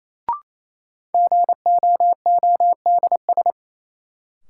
Morse code, heard as a keyed tone, about 28 wpm, 700 Hz. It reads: GOODS
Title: Sentences from Top 1000 Words - Review Part 6 - New Word Drill - 28wpm